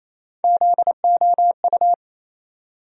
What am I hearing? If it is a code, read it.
Morse code, heard as a keyed tone, about 28 wpm, 700 Hz. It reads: ZOU
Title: USA Call Sign Postfixes - 3 Letters 28wpm